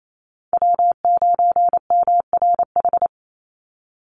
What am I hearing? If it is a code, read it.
Morse code, heard as a keyed tone, about 28 wpm, 700 Hz. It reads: W9MRH
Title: NA SOTA Chaser Callsigns 28wpm